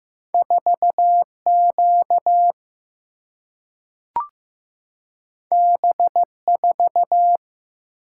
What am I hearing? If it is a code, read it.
Morse code, heard as a keyed tone, about 15 wpm, 700 Hz. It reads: 4 Q B 4
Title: Single-Letter Single-Number 15wpm